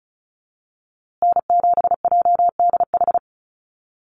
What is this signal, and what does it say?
Morse code, heard as a keyed tone, about 34 wpm, 700 Hz. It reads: N7JDH
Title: USA Call Signs Repeated 2x 35wpm